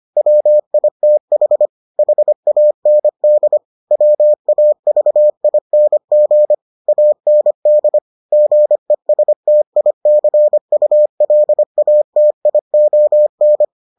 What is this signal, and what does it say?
Morse code, heard as a keyed tone, about 25 wpm, 600 Hz. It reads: WITH HAND WAVING AND GESTICULATION